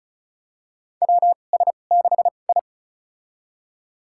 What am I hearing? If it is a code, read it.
Morse code, heard as a keyed tone, about 36 wpm, spaced as for 25 wpm, 700 Hz. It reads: WS6I
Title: USA Call Signs - 4 Characters - 2x Character Spacing - 35wpm